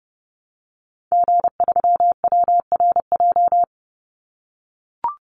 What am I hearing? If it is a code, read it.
Morse code, heard as a keyed tone, about 30 wpm, 700 Hz. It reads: G3WRJ
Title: Call Signs - 30wpm